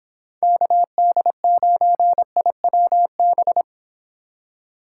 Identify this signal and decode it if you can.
Morse code, heard as a keyed tone, about 26 wpm, 700 Hz. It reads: KD9IWB